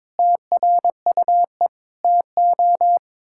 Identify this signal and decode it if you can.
Morse code, heard as a keyed tone, about 22 wpm, 700 Hz. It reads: TRUE TO
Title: Sentences from Top 1000 Words - Review Part 6 - 22wpm